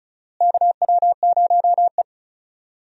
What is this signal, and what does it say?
Morse code, heard as a keyed tone, about 36 wpm, 700 Hz. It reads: KW0E
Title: USA Call Signs - 4 Characters - Repeated 3x - 35wpm